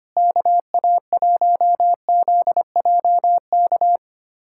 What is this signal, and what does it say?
Morse code, heard as a keyed tone, about 25 wpm, 700 Hz. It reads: KA1ZJK
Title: USA Callsign DE Callsign - 6 Characters - 25wpm